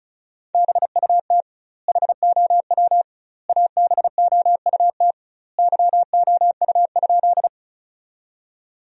Morse code, heard as a keyed tone, about 35 wpm, 700 Hz. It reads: BUT HOW ABOUT YOU?
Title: 2x Word-Spacing - Sentences from Top 100 Words - 35wpm